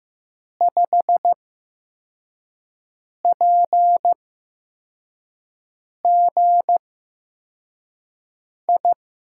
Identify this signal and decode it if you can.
Morse code, heard as a keyed tone, about 15 wpm, spaced as for 4 wpm, 700 Hz. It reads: 5PGI
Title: USA Call Signs - 6 Characters - 8x Character Spacing - 15wpm